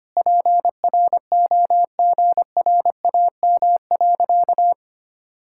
Morse code, heard as a keyed tone, about 25 wpm, 700 Hz. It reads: PROGRAM.
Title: Sentences from Top 1000 Words - Review Part 5 - 25wpm